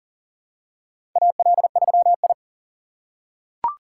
Morse code, heard as a keyed tone, about 40 wpm, 700 Hz. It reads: AL3I
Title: USA Call Signs - 4 Characters - 3x Character Spacing - 40wpm